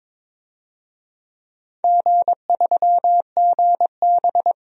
Morse code, heard as a keyed tone, about 21 wpm, 700 Hz. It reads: G3GB
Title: DXCC-100 Call Signs - 4 Characters - 22wpm